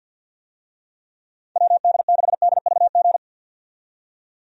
Morse code, heard as a keyed tone, about 50 wpm, 700 Hz. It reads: WD6DVG
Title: USA Call Signs - 6 Characters - 50wpm